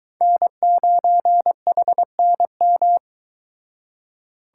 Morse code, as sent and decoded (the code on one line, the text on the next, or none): -. ----. .... -. --
N9HNM